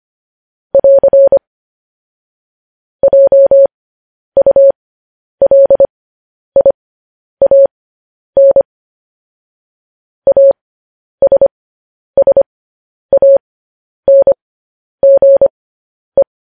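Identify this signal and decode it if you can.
Morse code, heard as a keyed tone, about 25 wpm, spaced as for 10 wpm, 550 Hz. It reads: <AR> JULIAN ASSANGE
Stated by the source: Morse code news headlines sent at 10.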